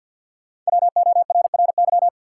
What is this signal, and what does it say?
Morse code, heard as a keyed tone, about 50 wpm, 700 Hz. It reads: WORRY
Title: Sentences from Top 1100 Words - New Word Drill - 50wpm